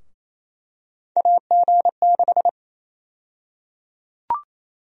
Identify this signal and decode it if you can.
Morse code, heard as a keyed tone, about 28 wpm, 700 Hz. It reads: AG6
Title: USA Call Sign Prefixes - 28wpm